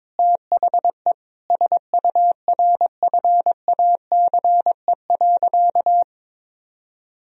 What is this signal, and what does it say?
Morse code, heard as a keyed tone, about 22 wpm, 700 Hz. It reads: THE SURFACE.
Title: Sentences from Top 1000 Words - Review Part 3 - 22wpm